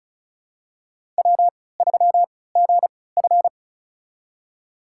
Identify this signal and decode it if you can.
Morse code, heard as a keyed tone, about 36 wpm, spaced as for 20 wpm, 700 Hz. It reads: W3GF
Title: USA Call Signs - 4 Characters - 3x Character Spacing - 35wpm